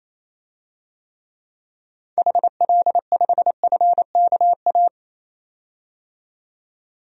Morse code, heard as a keyed tone, about 28 wpm, 700 Hz. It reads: HL5FKA
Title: DXCC-100 Call Signs - 6 Characters - 28wpm